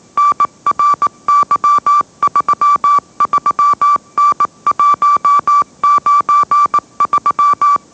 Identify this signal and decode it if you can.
Morse code, heard as a keyed tone, about 18 wpm, 1160 Hz. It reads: NRY33N193